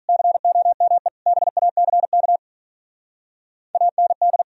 Morse code, heard as a keyed tone, about 43 wpm, 700 Hz. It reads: COME BACK AND
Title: Speed-Racing - Sets of 5 Words 35wpm